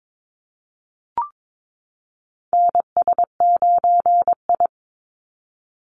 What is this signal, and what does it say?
Morse code, heard as a keyed tone, about 22 wpm, 700 Hz. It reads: NS9I
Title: Call Signs Repeated 2x - 22wpm